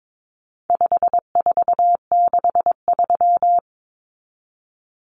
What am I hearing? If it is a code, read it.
Morse code, heard as a keyed tone, about 22 wpm, 700 Hz. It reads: 5463
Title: Numbers - 4-Digit 22wpm